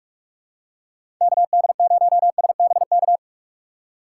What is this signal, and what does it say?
Morse code, heard as a keyed tone, about 46 wpm, 700 Hz. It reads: KD0SBK
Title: USA Call Signs - 6 Characters - 4x Character Spacing - 45wpm